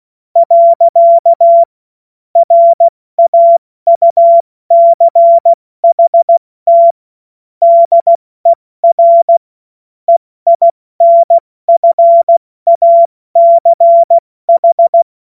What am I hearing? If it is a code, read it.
Morse code, heard as a keyed tone, about 16 wpm, 680 Hz. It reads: . RAUCHT DER EINFACH
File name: Morsekurs / Witze / Witz 27. Juli 2021, 70 F80